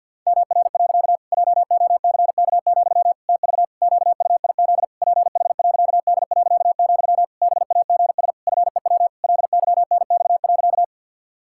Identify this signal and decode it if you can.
Morse code, heard as a keyed tone, about 50 wpm, 700 Hz. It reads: MR. JOCK, TV QUIZ PH.D., BAGS FEW LYNX.
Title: Pangram Challenge v2 - 50wpm